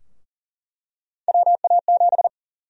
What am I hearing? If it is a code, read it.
Morse code, heard as a keyed tone, about 37 wpm, 700 Hz. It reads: WA7
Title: USA Call Sign Prefixes - Repeated 2x - 40wpm